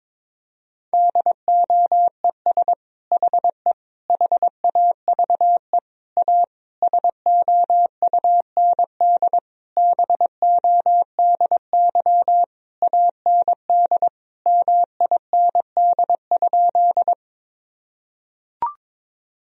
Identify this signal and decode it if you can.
Morse code, heard as a keyed tone, about 22 wpm, 700 Hz. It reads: DOES HE HAVE A SOUND BODY AND MIND?
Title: Sentences from Top 1000 Words - Review Part 2 - 22wpm